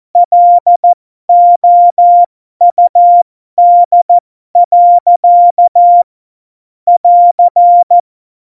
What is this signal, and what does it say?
Morse code, heard as a keyed tone, about 14 wpm, spaced as for 12 wpm, 700 Hz. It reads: LOUD. <AR>
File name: Morsekurs / Witze